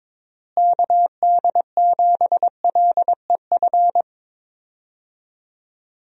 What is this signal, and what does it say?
Morse code, heard as a keyed tone, about 22 wpm, 700 Hz. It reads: KD7LEF
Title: USA Call Signs - 6 Characters - Repeated 2x - 22wpm